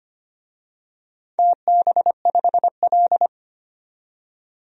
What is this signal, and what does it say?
Morse code, heard as a keyed tone, about 25 wpm, 700 Hz. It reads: TB5L
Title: DXCC-100 Call Signs - 4 Characters - 25wpm